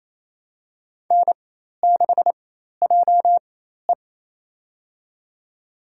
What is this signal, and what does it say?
Morse code, heard as a keyed tone, about 27 wpm, spaced as for 13 wpm, 700 Hz. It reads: N6JE
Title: USA Call Signs - 4 Characters - 4x Character Spacing - 28wpm